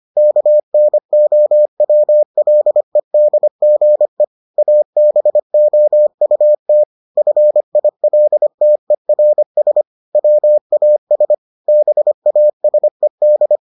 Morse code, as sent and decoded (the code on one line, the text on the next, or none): -.- -. --- .-- .-.. -.. --. / .- -... --- ..- - / ..-. .. .-.. - . .-. ... / .-- .- ... / -... .- ... -..
KNOWLEDGE ABOUT FILTERS WAS BASED